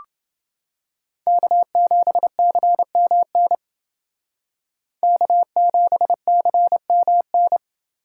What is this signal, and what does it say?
Morse code, heard as a keyed tone, about 28 wpm, 700 Hz. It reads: K7CMN K7CMN